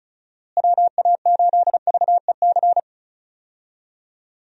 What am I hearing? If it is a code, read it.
Morse code, heard as a keyed tone, about 34 wpm, 700 Hz. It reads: WA8VEC
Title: USA Call Signs - 6 Characters - Repeated 2x - 35wpm